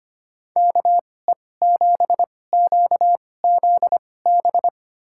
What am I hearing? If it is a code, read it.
Morse code, heard as a keyed tone, about 25 wpm, spaced as for 18 wpm, 700 Hz. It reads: KE7QZB
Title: USA Call Signs - 6 Characters - 2x Character Spacing - 25wpm